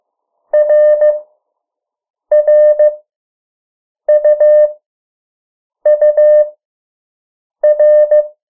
Morse code, heard as a keyed tone, about 15 wpm, spaced as for 6 wpm, 600 Hz. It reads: RRUUR
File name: morse mp3 / 03 urok 2